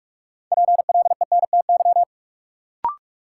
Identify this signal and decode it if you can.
Morse code, heard as a keyed tone, about 46 wpm, 700 Hz. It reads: PLENTY
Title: Sentences from Top 1300 Words - New Word Drill Alt - 45wpm